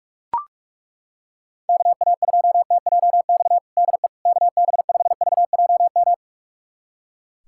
Morse code, heard as a keyed tone, about 42 wpm, 700 Hz. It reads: KA2TJX DE KB5VJM
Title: USA Callsign DE Callsign 45wpm